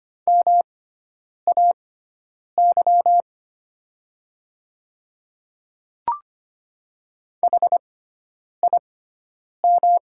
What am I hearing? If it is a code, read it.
Morse code, heard as a keyed tone, about 25 wpm, spaced as for 9 wpm, 700 Hz. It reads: MAY HIM
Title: 6x Character Spacing - Top 100 Words - 25wpm